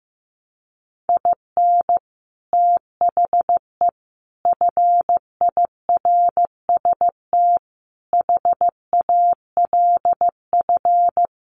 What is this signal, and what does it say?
Morse code, heard as a keyed tone, about 15 wpm, 700 Hz. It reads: IN THE FIRST HALF
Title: Sets of 5 Words 15wpm